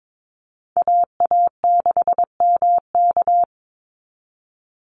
Morse code, heard as a keyed tone, about 22 wpm, 700 Hz. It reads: AA6MK
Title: Call Signs - 22wpm